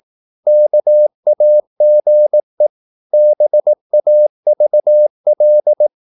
Morse code, heard as a keyed tone, about 18 wpm, 600 Hz. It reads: KAGE BAVL
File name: Korte ord DK 18wpm